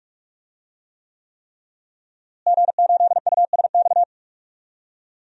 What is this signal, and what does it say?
Morse code, heard as a keyed tone, about 46 wpm, 700 Hz. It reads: G8USX
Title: DXCC-100 Call Signs - 5 Characters - Repeated 3x - 45wpm